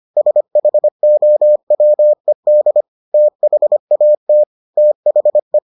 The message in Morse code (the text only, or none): SHOWED THAT THE